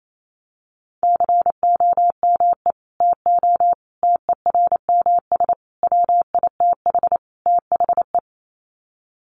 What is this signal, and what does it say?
Morse code, heard as a keyed tone, about 28 wpm, 700 Hz. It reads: COME TO TERMS WITH THE
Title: Sets of 5 Words Encore2 28wpm